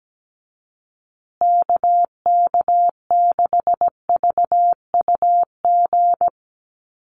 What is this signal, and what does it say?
Morse code, heard as a keyed tone, about 17 wpm, 700 Hz. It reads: KK6VUG
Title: USA Call Signs Repeated 3x 17wpm